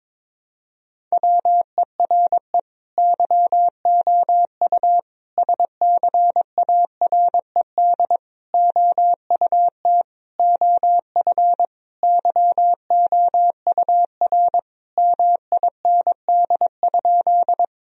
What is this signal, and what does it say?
Morse code, heard as a keyed tone, about 22 wpm, 700 Hz. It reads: WERE YOU SCARED OUT OF YOUR MIND?
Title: Sentences from Top 1000 Words - Review Part 4 - 22wpm